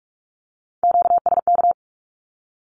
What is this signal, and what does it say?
Morse code, heard as a keyed tone, about 44 wpm, 700 Hz. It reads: QSK
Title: Q-Codes 45wpm